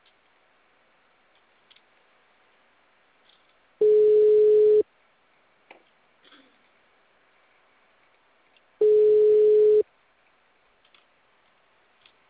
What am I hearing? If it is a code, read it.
Ringback tone (Europe, 425 Hz).